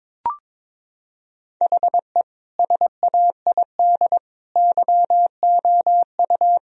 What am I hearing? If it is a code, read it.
Morse code, heard as a keyed tone, about 21 wpm, 700 Hz. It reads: HE SAID YOU
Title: Sentences from Top 1000 Words - Review Part 2 - 22wpm